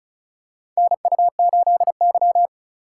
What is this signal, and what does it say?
Morse code, heard as a keyed tone, about 36 wpm, 700 Hz. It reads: NU8Y
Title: USA Call Signs - 3x Character Spacing - 35wpm